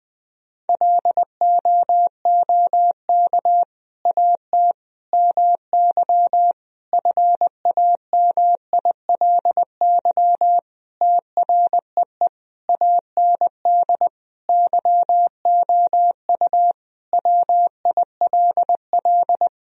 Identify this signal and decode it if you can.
Morse code, heard as a keyed tone, about 20 wpm, 700 Hz. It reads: LOOK AT MY FAMILY TREE AND YOU WILL
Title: Sentences from Top 1000 Words - Review Part 1 - 20wpm